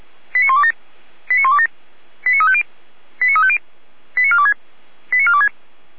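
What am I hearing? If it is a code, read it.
Five-tone selective call (ZVEI), heard as quick runs of five tones, tones of about 70 ms. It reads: 89127 89127 89350 89350 89426 89426
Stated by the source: ZVEI five-tone selective call received with an SDR on a fire brigade channel, directory labelled 455.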